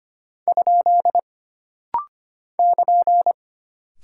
Morse code, heard as a keyed tone, about 25 wpm, 700 Hz. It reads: ? <KN>
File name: Single Punctuation-Prosign - Rapid-Fire - 25wpm